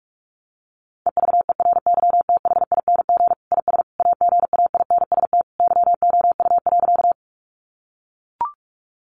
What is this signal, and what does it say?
Morse code, heard as a keyed tone, about 44 wpm, 700 Hz. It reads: EVERYTHING IS AGAINST YOU.
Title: Sentences from Top 700 Words 45wpm